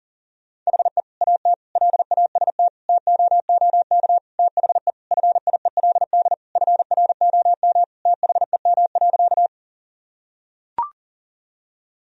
Morse code, heard as a keyed tone, about 40 wpm, 700 Hz. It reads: HE AT LAST TOOK THE FIELD FROM THEM.
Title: Sentences from Top 1000 Words - Review Part 2 - 40wpm